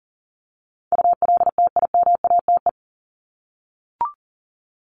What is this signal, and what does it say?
Morse code, heard as a keyed tone, about 40 wpm, 700 Hz. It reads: ULTIMATE
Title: LY Suffix - Top 100 Words with Base Words 40wpm